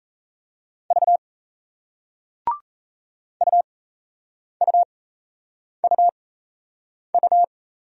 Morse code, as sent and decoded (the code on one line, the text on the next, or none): ...- / ..- ..- ..- ..-
V UUUU